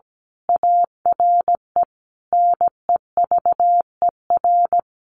Morse code, heard as a keyed tone, about 17 wpm, 700 Hz. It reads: ARE NEVER
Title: A Sound Of Thunder 17wpm